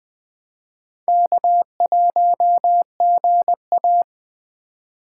Morse code, heard as a keyed tone, about 20 wpm, 700 Hz. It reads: K1GA